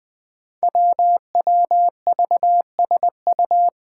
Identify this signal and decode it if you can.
Morse code, heard as a keyed tone, about 20 wpm, 700 Hz. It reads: WWVSU
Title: Five Challenging Letter-Number Combinations 20wpm